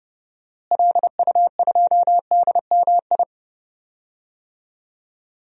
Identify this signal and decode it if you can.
Morse code, heard as a keyed tone, about 30 wpm, 700 Hz. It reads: LU2DMI
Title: DXCC-100 Call Signs - 6 Characters - Repeated 3x - 30wpm